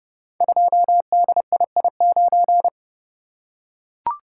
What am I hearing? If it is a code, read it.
Morse code, heard as a keyed tone, about 30 wpm, 700 Hz. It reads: 2DII9
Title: Code Groups - 5 Letter:Numbers - 30wpm